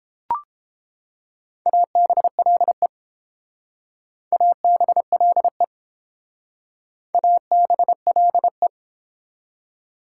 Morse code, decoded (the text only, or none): ABLE ABLE ABLE